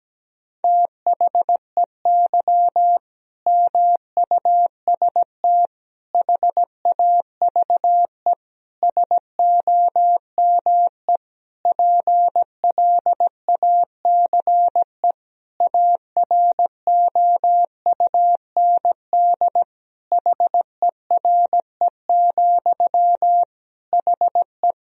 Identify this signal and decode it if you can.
Morse code, heard as a keyed tone, about 17 wpm, 700 Hz. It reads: THEY MUST HAVE SOME PLACE AROUND HERE, HE
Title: Sentences from Top 1000 Words - Review Part 3 - 17wpm